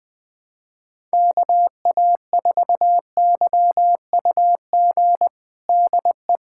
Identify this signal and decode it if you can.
Morse code, heard as a keyed tone, about 21 wpm, 700 Hz. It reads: KA4YUG DE
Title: USA Callsign DE Callsign - 6 Characters - 20wpm